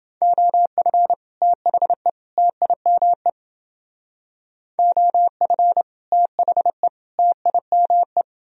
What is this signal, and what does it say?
Morse code, heard as a keyed tone, about 28 wpm, 700 Hz. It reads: OF THE TIME OF THE TIME